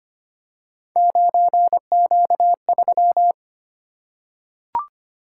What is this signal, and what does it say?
Morse code, heard as a keyed tone, about 25 wpm, 700 Hz. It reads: 9Q3
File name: Three Letter-Number Combinations - 25wpm